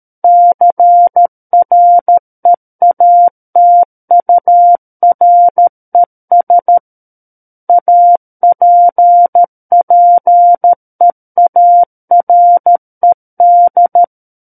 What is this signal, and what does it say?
Morse code, heard as a keyed tone, about 13 wpm, 700 Hz. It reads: CREATURES APPEARED
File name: Book One / 04-Chapter 4 - 13 WPM